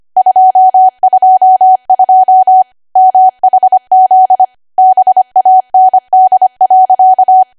Morse code, heard as a keyed tone, about 25 wpm, 750 Hz. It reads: 222 MHZ BAND.
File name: Archive / 25 WPM / 25WPM